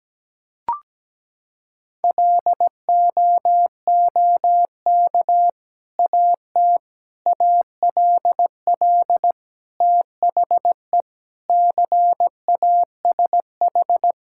Morse code, heard as a keyed tone, about 17 wpm, 700 Hz. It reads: LOOK AT ALL THE CASH
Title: Sentences from Top 1000 Words - Review Part 5 - 17wpm